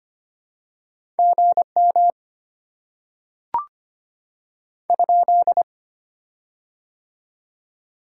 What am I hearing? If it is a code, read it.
Morse code, heard as a keyed tone, about 25 wpm, 700 Hz. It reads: GM ?
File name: Top 90 QSO Elements - 6x Character Spacing - 25wpm